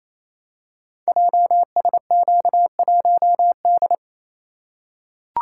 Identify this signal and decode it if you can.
Morse code, heard as a keyed tone, about 29 wpm, 700 Hz. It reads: JSQ1D
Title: Code Groups - 5 Letter:Numbers - 28wpm